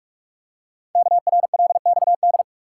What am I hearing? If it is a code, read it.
Morse code, heard as a keyed tone, about 44 wpm, 700 Hz. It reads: KRLXD
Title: Code Groups - 5 Letters - 45wpm